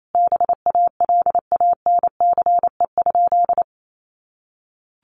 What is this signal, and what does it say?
Morse code, heard as a keyed tone, about 28 wpm, 700 Hz. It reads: BALANCE?
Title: Sentences from Top 900 Words 28wpm